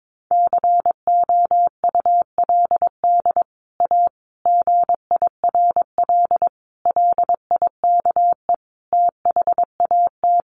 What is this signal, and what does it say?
Morse code, heard as a keyed tone, about 22 wpm, 700 Hz. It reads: COULD A GIRL LIKE THAT
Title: Sentences from Top 1200 Words 22wpm